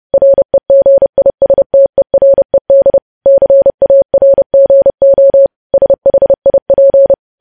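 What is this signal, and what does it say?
Morse code, heard as a keyed tone, about 30 wpm, 550 Hz. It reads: REGISTERED CARGO SHIP
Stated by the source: News headlines sent in Morse at 30.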